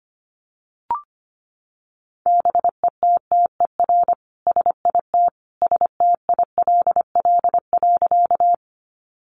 Morse code, heard as a keyed tone, about 25 wpm, 700 Hz. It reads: BETTER SIT STILL.
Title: Sentences from Top 500 Words 25wpm